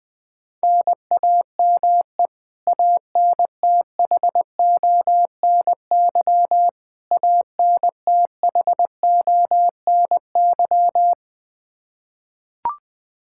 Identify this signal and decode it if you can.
Morse code, heard as a keyed tone, about 21 wpm, 700 Hz. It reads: NAME ANTHONY ANTHONY
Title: Practice QSO - Exchange 1 - Beginning - 20wpm